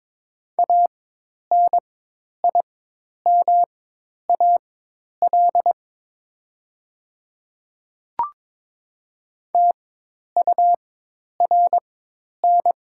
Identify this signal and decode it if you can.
Morse code, heard as a keyed tone, about 23 wpm, spaced as for 10 wpm, 700 Hz. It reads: ANIMAL TURN
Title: 4x Character Spacing - Top 500 Words - 22wpm